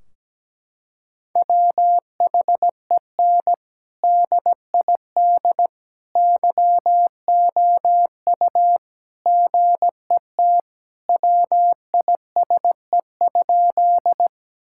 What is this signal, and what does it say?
Morse code, heard as a keyed tone, about 17 wpm, 700 Hz. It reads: WHEN DID YOU GET WISE?
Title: Sentences from Top 1600 Words 17wpm